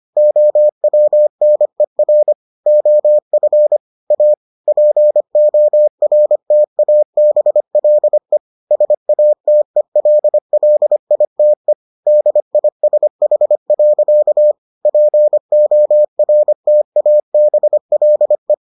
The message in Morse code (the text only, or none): OWNER OF A PORTABLE SATELLITE DISH. PORTABLE